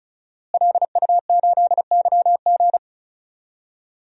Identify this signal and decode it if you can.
Morse code, heard as a keyed tone, about 34 wpm, 700 Hz. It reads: LU8YG
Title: DXCC-100 Call Signs - 5 Characters - Repeated 3x - 35wpm